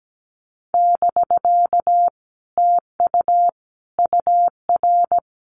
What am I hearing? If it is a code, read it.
Morse code, heard as a keyed tone, about 17 wpm, 700 Hz. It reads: <BK> TU UR
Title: Common QSO Phrases Encore 17wpm